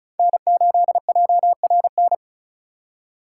Morse code, heard as a keyed tone, about 36 wpm, 700 Hz. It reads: N8JRN
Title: USA Call Signs - 5 Characters - 4x Character Spacing - 35wpm